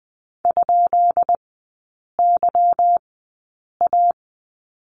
Morse code, heard as a keyed tone, about 20 wpm, spaced as for 8 wpm, 700 Hz. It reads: ?YA
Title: 159 Lesson 26e - Letter g - Three Characters 20wpm